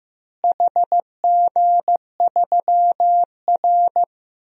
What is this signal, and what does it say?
Morse code, heard as a keyed tone, about 15 wpm, 700 Hz. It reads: HG3R
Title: DXCC-100 Call Signs - 4 Characters - Repeated 2x - 15wpm